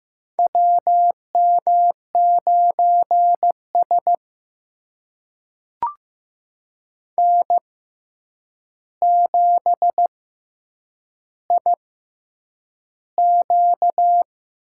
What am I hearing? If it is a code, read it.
Morse code, heard as a keyed tone, about 15 wpm, spaced as for 8 wpm, 700 Hz. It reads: WM9S N7IQ
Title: USA Call Signs - 4 Characters - 6x Character Spacing - 15wpm